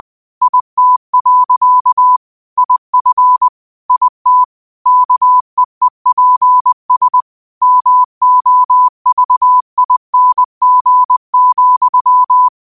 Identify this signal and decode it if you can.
Morse code, heard as a keyed tone, about 20 wpm, 1000 Hz. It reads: IT. IF IT KEEPS MOVING,
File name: qotdfiles / 20wpm